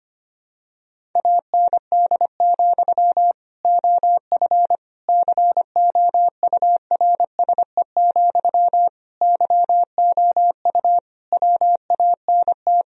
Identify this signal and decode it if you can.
Morse code, heard as a keyed tone, about 25 wpm, 700 Hz. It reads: AND, OF COURSE, YOU WANT
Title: Sentences from Top 1000 Words - Review Part 1 - 25wpm